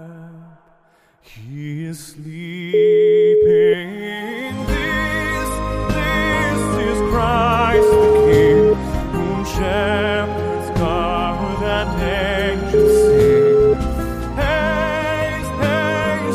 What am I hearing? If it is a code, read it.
Ringback tone (Europe, 425 Hz), with noise.